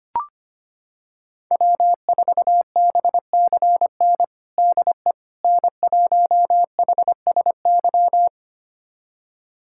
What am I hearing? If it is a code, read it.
Morse code, heard as a keyed tone, about 25 wpm, 700 Hz. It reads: W4BCN DE N1HSY
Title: USA Callsign DE Callsign - 5 Characters - 25wpm